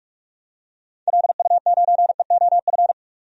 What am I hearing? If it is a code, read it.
Morse code, heard as a keyed tone, about 44 wpm, 700 Hz. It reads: LU9EOF
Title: DXCC-100 Call Signs - 6 Characters - 45wpm